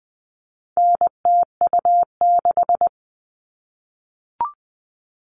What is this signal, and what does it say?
Morse code, heard as a keyed tone, about 20 wpm, 700 Hz. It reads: NTU6
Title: All Cut Numbers - 4-Digit 20wpm